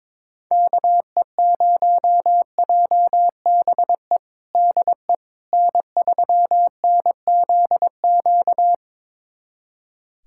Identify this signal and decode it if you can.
Morse code, heard as a keyed tone, about 22 wpm, 700 Hz. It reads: KE0JBE DE N3NZQ
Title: USA Callsign DE Callsign 22wpm